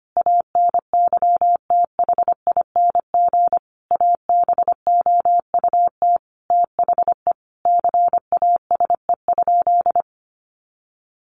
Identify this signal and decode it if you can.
Morse code, heard as a keyed tone, about 25 wpm, 700 Hz. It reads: ANYTHING ABOUT THE CASE?
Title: Sentences from Top 1000 Words 25wpm